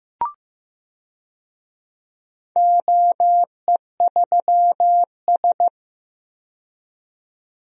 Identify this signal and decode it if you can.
Morse code, heard as a keyed tone, about 15 wpm, 700 Hz. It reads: OE3S
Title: DXCC-100 Call Signs - 4 Characters - 15wpm